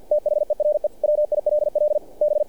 Morse code, heard as a keyed tone, about 48 wpm, 600 Hz. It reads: THERE WILL B
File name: archived / CRSnet